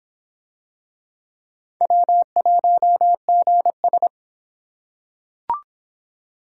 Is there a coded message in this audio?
Yes: Morse code from 1.8 to 4.1 s, heard as a keyed tone, about 25 wpm, 700 Hz. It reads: W1GS